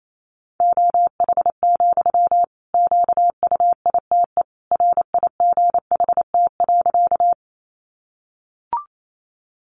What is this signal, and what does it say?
Morse code, heard as a keyed tone, about 28 wpm, 700 Hz. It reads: OH, QUITE RIGHT.
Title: Sentences from Top 700 Words 28wpm